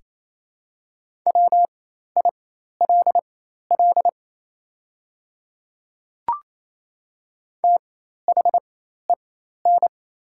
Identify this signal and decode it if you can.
Morse code, heard as a keyed tone, about 28 wpm, spaced as for 13 wpm, 700 Hz. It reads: WILL THEN
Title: 4x Character Spacing - Top 200 Words - 28wpm